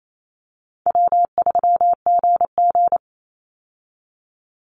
Morse code, heard as a keyed tone, about 28 wpm, 700 Hz. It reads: W3GG
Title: Easy Call Signs Repeated 3x - 28wpm